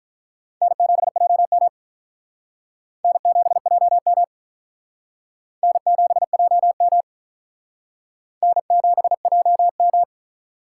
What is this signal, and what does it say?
Morse code, heard as a keyed tone, about 48 wpm, 700 Hz. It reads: N7JM N7JM N7JM N7JM